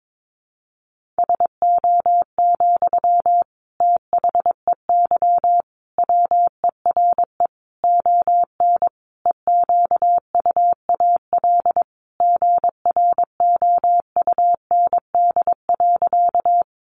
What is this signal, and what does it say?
Morse code, heard as a keyed tone, about 22 wpm, 700 Hz. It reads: SO, THEY WERE ON EQUAL GROUND.
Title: Sentences from Top 1400 Words 22wpm